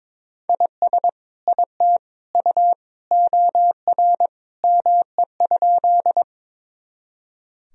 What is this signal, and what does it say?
Morse code, heard as a keyed tone, about 22 wpm, 700 Hz. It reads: IS IT U OR ME?
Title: Sentences from Top 200 Words and Abbreviations 22wpm